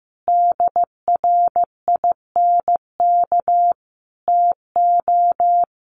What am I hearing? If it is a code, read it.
Morse code, heard as a keyed tone, about 15 wpm, 700 Hz. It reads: DRINK TO
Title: Sentences from Top 1500 Words 15wpm